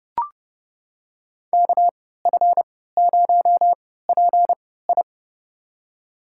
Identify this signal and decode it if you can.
Morse code, heard as a keyed tone, about 29 wpm, spaced as for 17 wpm, 700 Hz. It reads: KF0PI
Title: USA Call Signs - 5 Characters - 3x Character Spacing - 30wpm